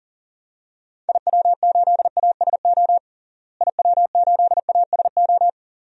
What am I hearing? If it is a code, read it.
Morse code, heard as a keyed tone, about 40 wpm, 700 Hz. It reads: IW8ASO IW8ASO
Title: DXCC-100 Call Signs - 6 Characters - Repeated 2x - 40wpm